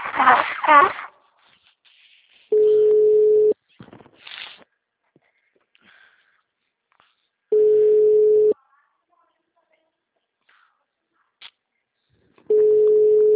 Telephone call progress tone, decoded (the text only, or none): ringback tone